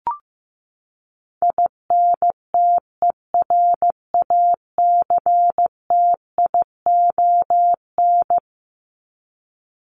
Morse code, heard as a keyed tone, about 15 wpm, 700 Hz. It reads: INTERACTION
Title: Top 1251-1500 Words 15wpm